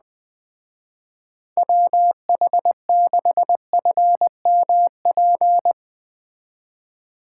Morse code, heard as a keyed tone, about 20 wpm, 700 Hz. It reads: WH6FMP
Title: Speed Racing - USA 2x3 Call Signs 17wpm